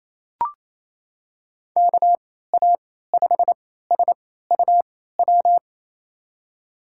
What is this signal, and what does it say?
Morse code, heard as a keyed tone, about 28 wpm, spaced as for 16 wpm, 700 Hz. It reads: KA5SUW
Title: USA Call Signs - 6 Characters - 3x Character Spacing - 28wpm